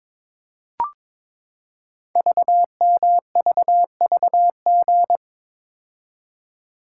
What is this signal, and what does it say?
Morse code, heard as a keyed tone, about 22 wpm, 700 Hz. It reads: VMVVG
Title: Code Groups - 5 Letters - 22wpm